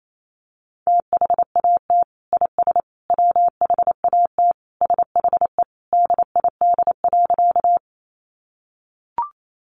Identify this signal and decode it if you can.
Morse code, heard as a keyed tone, about 28 wpm, 700 Hz. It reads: THAT IS WHAT SHE DID.
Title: Sentences from Top 100 Words 28wpm